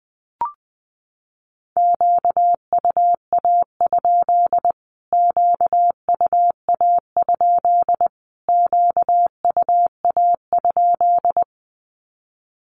Morse code, heard as a keyed tone, about 20 wpm, 700 Hz. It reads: QUA? QUA? QUA?
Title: Q-Codes Repeated 3x 20wpm